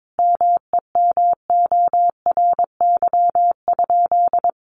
Morse code, heard as a keyed tone, about 22 wpm, 700 Hz. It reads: MEMORY?
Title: Sentences from Top 1300 Words 22wpm